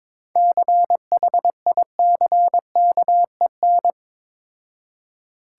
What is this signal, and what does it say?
Morse code, heard as a keyed tone, about 21 wpm, 690 Hz. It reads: CHICKEN
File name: Difficult Words with Hints - 22wpm